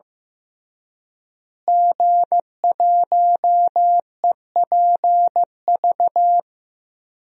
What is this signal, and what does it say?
Morse code, heard as a keyed tone, about 15 wpm, 700 Hz. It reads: G1EPV
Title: DXCC-100 Call Signs - 5 Characters - Repeated 2x - 15wpm